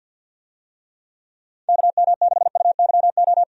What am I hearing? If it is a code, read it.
Morse code, heard as a keyed tone, about 51 wpm, 700 Hz. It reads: KM6UYO
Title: USA Call Signs - 6 Characters - 50wpm